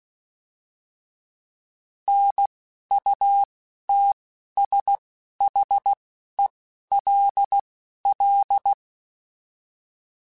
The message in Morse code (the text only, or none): NUTSHELL